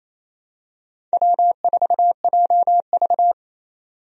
Morse code, heard as a keyed tone, about 28 wpm, 700 Hz. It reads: W4JV
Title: USA Call Signs - 4 Characters - 28wpm